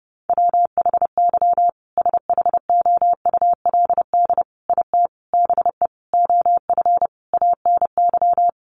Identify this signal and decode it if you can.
Morse code, heard as a keyed tone, about 28 wpm, 700 Hz. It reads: WHY SHOULD IT BE OF ANY
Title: Sentences from Top 1200 Words 30wpm